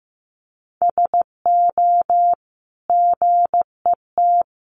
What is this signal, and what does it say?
Morse code, heard as a keyed tone, about 15 wpm, 700 Hz. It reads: SO GET
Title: Sentences from Top 1000 Words 15wpm